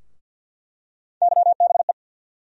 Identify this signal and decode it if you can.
Morse code, heard as a keyed tone, about 47 wpm, 700 Hz. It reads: YBE
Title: USA Call Sign Postfixes - 3 Letters 50wpm